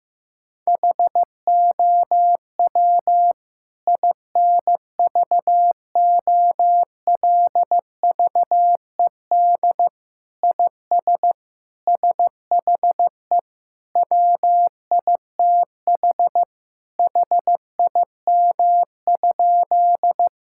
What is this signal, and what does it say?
Morse code, heard as a keyed tone, about 16 wpm, 700 Hz. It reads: HOW INVOLVED IS SHE WITH HIM?
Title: Sentences from Top 1000 Words - Review Part 5 - 15wpm